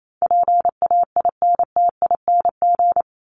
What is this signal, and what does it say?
Morse code, heard as a keyed tone, about 28 wpm, 700 Hz. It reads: PAINTING
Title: Top 751-1000 Words 28wpm